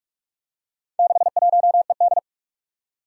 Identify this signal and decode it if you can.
Morse code, heard as a keyed tone, about 46 wpm, 700 Hz. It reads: 61ED
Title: Four Challenging Letter-Number Combinations 45wpm